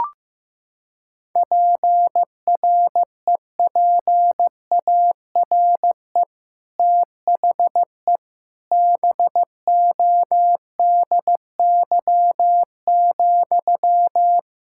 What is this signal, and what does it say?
Morse code, heard as a keyed tone, about 16 wpm, 700 Hz. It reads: PREPARE THE BODY,
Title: Sentences from Top 1000 Words - Review Part 6 - 15wpm